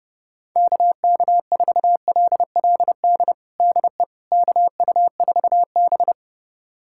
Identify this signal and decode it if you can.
Morse code, heard as a keyed tone, about 28 wpm, 700 Hz. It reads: KK4LLD DE KU4B
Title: USA Callsign DE Callsign 30wpm